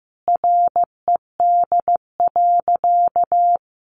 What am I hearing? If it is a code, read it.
Morse code, heard as a keyed tone, about 15 wpm, 700 Hz. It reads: RED.
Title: Easy Phrases 15wpm